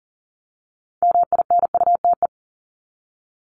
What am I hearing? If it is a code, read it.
Morse code, heard as a keyed tone, about 40 wpm, 700 Hz. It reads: MINUTE
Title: ES:S Suffix - Top 100 Words with Base Words 40wpm